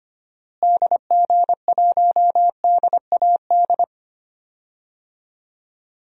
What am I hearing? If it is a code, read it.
Morse code, heard as a keyed tone, about 25 wpm, 700 Hz. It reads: DG1DAD
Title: DXCC-100 Call Signs - 6 Characters - Repeated 2x - 25wpm